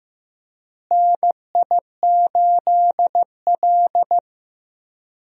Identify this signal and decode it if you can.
Morse code, heard as a keyed tone, about 16 wpm, 700 Hz. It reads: NI8L
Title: USA Call Signs - 4 Characters - Repeated 3x - 15wpm